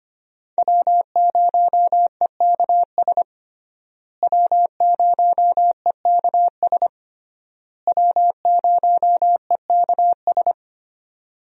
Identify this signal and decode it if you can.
Morse code, heard as a keyed tone, about 25 wpm, 700 Hz. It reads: W0EKS W0EKS W0EKS
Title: USA Call Signs - 5 Characters - Repeated 3x - 25wpm